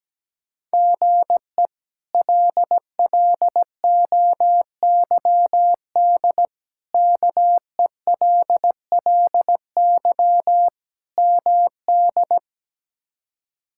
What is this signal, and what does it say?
Morse code, heard as a keyed tone, about 17 wpm, 700 Hz. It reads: GE LLOYD KELLY MD
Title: K1USN SST QSO Practice - 17wpm